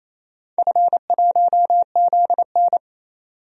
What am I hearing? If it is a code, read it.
Morse code, heard as a keyed tone, about 28 wpm, 700 Hz. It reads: F1ZN
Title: DXCC-100 Call Signs - 4 Characters - 28wpm